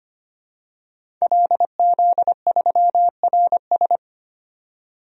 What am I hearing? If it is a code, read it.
Morse code, heard as a keyed tone, about 25 wpm, 700 Hz. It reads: LZ3RS
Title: DXCC-100 Call Signs - 5 Characters - 25wpm